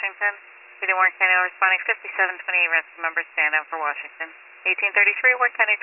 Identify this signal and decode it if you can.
Emergency Alert System (SAME) tones (header bursts), heard as harsh buzzing bursts, with speech. It is heard from 1.0 to 5.4 s.